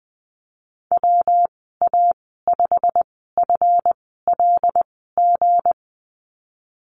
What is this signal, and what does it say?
Morse code, heard as a keyed tone, about 20 wpm, spaced as for 15 wpm, 700 Hz. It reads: WA5FLG
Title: 167 Lesson 27g - Number 7 - Call Signs 20wpm